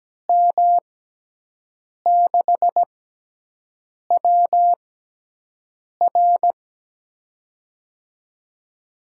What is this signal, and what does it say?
Morse code, heard as a keyed tone, about 17 wpm, spaced as for 6 wpm, 700 Hz. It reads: M6WR
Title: USA Call Signs - 5 Characters - 6x Character Spacing - 17wpm